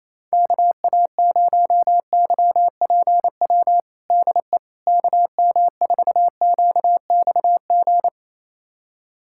Morse code, heard as a keyed tone, about 28 wpm, 700 Hz. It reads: KA0YPW DE KM4QXG
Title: USA Callsign DE Callsign - 6 Characters - 28wpm